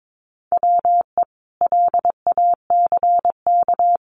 Morse code, heard as a keyed tone, about 22 wpm, 700 Hz. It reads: WE LACK
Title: Sentences from Top 1400 Words 22wpm